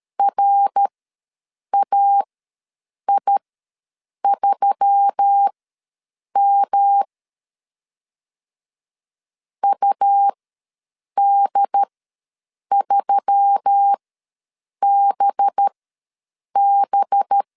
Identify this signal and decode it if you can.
Morse code, heard as a keyed tone, about 13 wpm, spaced as for 7 wpm, 800 Hz. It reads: RAI3M UD3BB
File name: 5 LB ws 12wpm